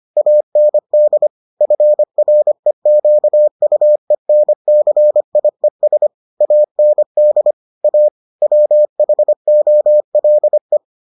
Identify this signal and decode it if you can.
Morse code, heard as a keyed tone, about 25 wpm, 600 Hz. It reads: AND FREQUENCIES AND A WHOLE